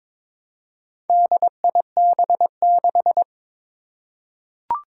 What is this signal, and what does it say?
Morse code, heard as a keyed tone, about 22 wpm, 700 Hz. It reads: DIB6
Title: Four Challenging Letter-Number Combinations 22wpm